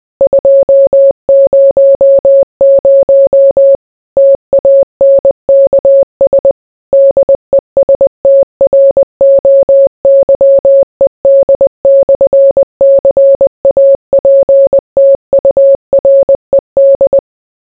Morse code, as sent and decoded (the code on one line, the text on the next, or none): ..--- ----- ----- / - .- -. -.- ... / -.. ... - .-. --- -.-- . -.. -..-. -.-. .- .--. - ..- .-. -..
200 TANKS DESTROYED/CAPTURED